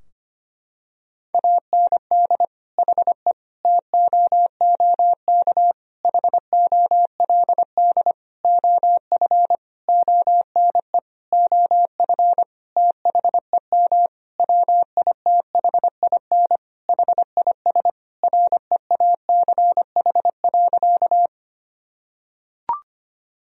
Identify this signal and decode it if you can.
Morse code, heard as a keyed tone, about 25 wpm, 700 Hz. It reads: AND HE TOOK HOLD OF ONE OF THEM WITHIN HIS REACH.
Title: Sentences from Top 1000 Words - Review Part 4 - 25wpm